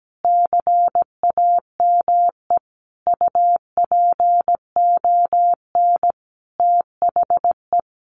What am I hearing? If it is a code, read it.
Morse code, heard as a keyed tone, about 17 wpm, 700 Hz. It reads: CAME UPON THE
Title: Sentences from Top 1100 Words 17wpm